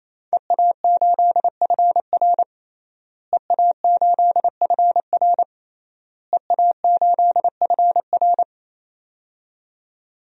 Morse code, heard as a keyed tone, about 28 wpm, 700 Hz. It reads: EA8FR EA8FR EA8FR
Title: DXCC-100 Call Signs - 5 Characters - Repeated 3x - 28wpm